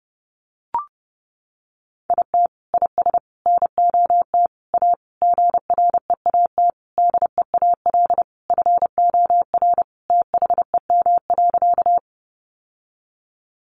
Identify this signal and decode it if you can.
Morse code, heard as a keyed tone, about 30 wpm, 700 Hz. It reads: IT IS NOT A GREAT DEAL FOR THEM.
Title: Sentences from Top 800 Words 30wpm